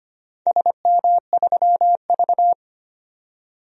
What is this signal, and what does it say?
Morse code, heard as a keyed tone, about 25 wpm, 700 Hz. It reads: SM3V
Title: DXCC-100 Call Signs - 4 Characters - 25wpm